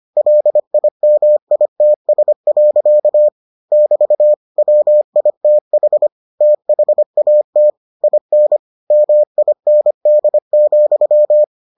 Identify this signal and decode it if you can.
Morse code, heard as a keyed tone, about 25 wpm, 600 Hz. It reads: LIMITS. <BT> WITH THAT IN MIND,